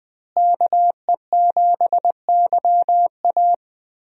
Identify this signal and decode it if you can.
Morse code, heard as a keyed tone, about 20 wpm, 700 Hz. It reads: KE7YA
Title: USA Call Signs - 5 Characters - Repeated 3x - 20wpm